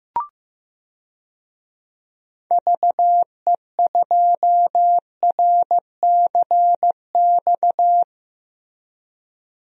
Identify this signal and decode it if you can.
Morse code, heard as a keyed tone, about 15 wpm, 700 Hz. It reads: VE2RCX
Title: DXCC-100 Call Signs - 6 Characters - Repeated 3x - 15wpm